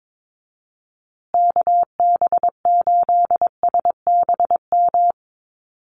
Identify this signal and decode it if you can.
Morse code, heard as a keyed tone, about 22 wpm, 700 Hz. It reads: KB8SBM
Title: USA Call Signs Repeated 2x 22wpm